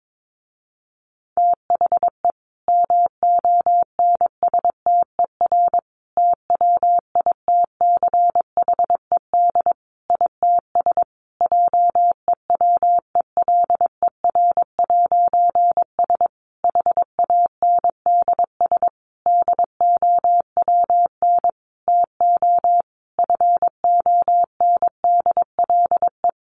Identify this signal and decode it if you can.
Morse code, heard as a keyed tone, about 22 wpm, 700 Hz. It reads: THE MONSTER TWITCHED ITS JEWELER'S HANDS DOWN TO FONDLE
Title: A Sound Of Thunder 22wpm